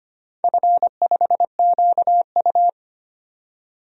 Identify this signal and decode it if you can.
Morse code, heard as a keyed tone, about 25 wpm, 700 Hz. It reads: F5QU
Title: DXCC-100 Call Signs - 4 Characters - 25wpm